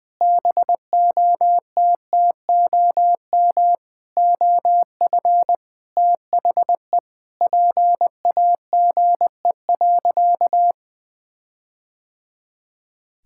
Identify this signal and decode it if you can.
Morse code, heard as a keyed tone, about 21 wpm, 700 Hz. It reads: BOTTOM OF THE PAGE.
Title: Sentences with 4-Word Phrases - 20wpm